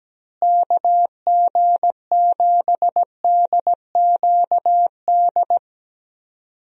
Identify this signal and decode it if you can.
Morse code, heard as a keyed tone, about 17 wpm, 700 Hz. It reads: KG7DQD
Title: USA Call Signs - 6 Characters - Repeated 2x - 17wpm